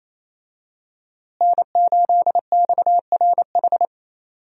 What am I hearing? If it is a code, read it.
Morse code, heard as a keyed tone, about 28 wpm, 700 Hz. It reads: N8XRH
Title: USA Call Signs - 5 Characters - Repeated 3x - 28wpm